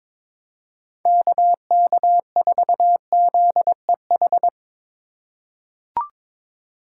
Morse code, heard as a keyed tone, about 22 wpm, 700 Hz. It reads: KK4ZEH
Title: USA Call Signs - 6 Characters - 2x Character Spacing - 22wpm